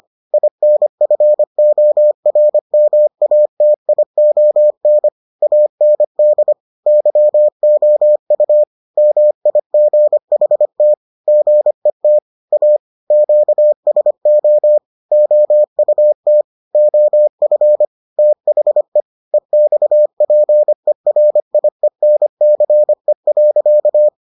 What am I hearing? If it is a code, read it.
Morse code, heard as a keyed tone, about 25 wpm, 600 Hz. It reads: INFORMATION AND YOU MIGHT GET A QSO OUT OF THE EXPERIENCE.